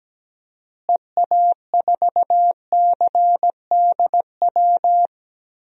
Morse code, heard as a keyed tone, about 17 wpm, 700 Hz. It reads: EA4CDW
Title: DXCC-100 Call Signs - 6 Characters - Repeated 3x - 17wpm